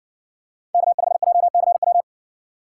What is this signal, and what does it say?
Morse code, heard as a keyed tone, about 60 wpm, 700 Hz. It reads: K5JCW